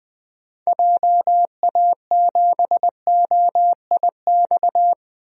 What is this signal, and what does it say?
Morse code, heard as a keyed tone, about 20 wpm, 700 Hz. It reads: JA7OIX
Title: DXCC-100 Call Signs - 6 Characters - 20wpm